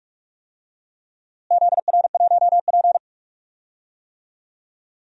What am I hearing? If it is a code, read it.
Morse code, heard as a keyed tone, about 45 wpm, 690 Hz. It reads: ZR1P
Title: DXCC-100 Call Signs - 4 Characters - Repeated 3x - 45wpm